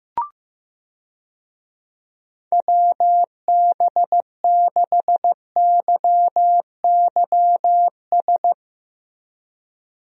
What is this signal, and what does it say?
Morse code, heard as a keyed tone, about 15 wpm, 700 Hz. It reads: WB6YYS
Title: USA Call Signs - 6 Characters - Repeated 2x - 15wpm